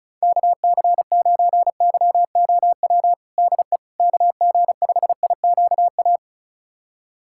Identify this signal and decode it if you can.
Morse code, heard as a keyed tone, about 34 wpm, 700 Hz. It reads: KC9YOW DE KG5IQA
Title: USA Callsign DE Callsign - 6 Characters - 35wpm